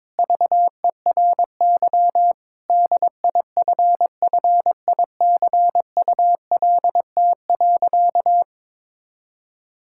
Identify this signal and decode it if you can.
Morse code, heard as a keyed tone, about 22 wpm, 700 Hz. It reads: VERY DIFFICULT.
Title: Sentences from Top 1000 Words - Review Part 6 - 22wpm